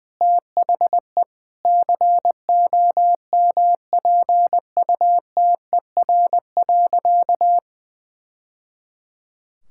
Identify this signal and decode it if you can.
Morse code, heard as a keyed tone, about 20 wpm, 700 Hz. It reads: THE COMPUTER.
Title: Sentences from Top 1000 Words - Review Part 6 - 20wpm